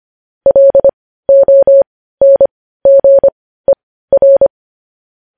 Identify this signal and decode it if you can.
Morse code, heard as a keyed tone, about 25 wpm, spaced as for 15 wpm, 550 Hz. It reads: LONGER
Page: News Headlines in Morse Code at 15 WPM